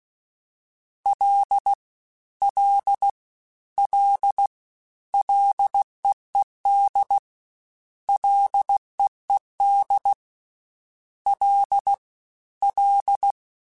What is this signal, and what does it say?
Morse code, heard as a keyed tone, about 16 wpm, 800 Hz. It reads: L L L LEED LEED L L
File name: F and L / F and L mp3 / F and L Prefixes 3x2 Twice @15